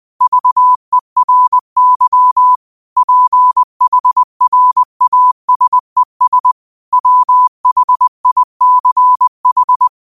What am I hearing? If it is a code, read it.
Morse code, heard as a keyed tone, about 20 wpm, 1000 Hz. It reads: VERY PHRASES WHICH